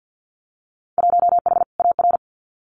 Morse code, heard as a keyed tone, about 50 wpm, 700 Hz. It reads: 1H AR
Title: Winter Field Day Exchanges 50wpm